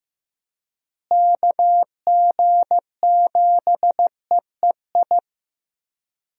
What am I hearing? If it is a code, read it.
Morse code, heard as a keyed tone, about 15 wpm, 700 Hz. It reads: KG7EEI
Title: USA Call Signs - 6 Characters - Repeated 2x - 15wpm